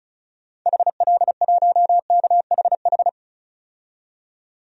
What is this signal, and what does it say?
Morse code, heard as a keyed tone, about 36 wpm, 700 Hz. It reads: HL1KHH
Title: DXCC-100 Call Signs - 6 Characters - 35wpm